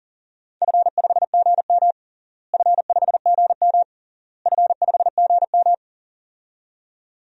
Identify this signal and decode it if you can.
Morse code, heard as a keyed tone, about 40 wpm, 700 Hz. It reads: F5GM F5GM F5GM
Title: DXCC-100 Call Signs - 4 Characters - Repeated 3x - 40wpm